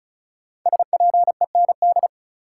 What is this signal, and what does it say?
Morse code, heard as a keyed tone, about 36 wpm, 700 Hz. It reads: SPEND
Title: Sets of 5 Words - New Word Drill - 35wpm